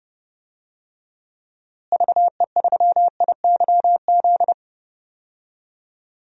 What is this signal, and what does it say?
Morse code, heard as a keyed tone, about 30 wpm, 700 Hz. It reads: VE3IYZ
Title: DXCC-100 Call Signs - 6 Characters - Repeated 3x - 30wpm